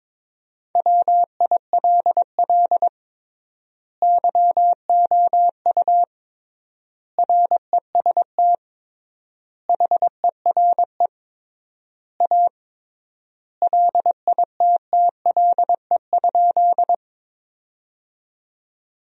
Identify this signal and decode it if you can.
Morse code, heard as a keyed tone, about 22 wpm, 700 Hz. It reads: WILL YOU REST HERE A LITTLE?
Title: 3x Word-Spacing - Sentences from Top 500 Words - 22wpm